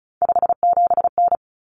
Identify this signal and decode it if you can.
Morse code, heard as a keyed tone, about 34 wpm, 700 Hz. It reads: 57N
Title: Common Cut Numbers - 3-Digit 35wpm